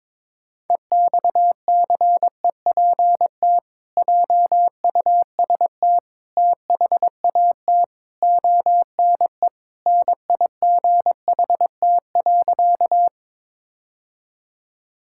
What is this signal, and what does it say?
Morse code, heard as a keyed tone, about 22 wpm, 700 Hz. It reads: EXCEPT JUST THAT ONE NIGHT.
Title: Sentences from Top 1600 Words 22wpm